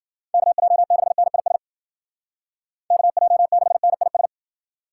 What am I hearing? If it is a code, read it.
Morse code, heard as a keyed tone, about 53 wpm, 700 Hz. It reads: KJ6NIS KJ6NIS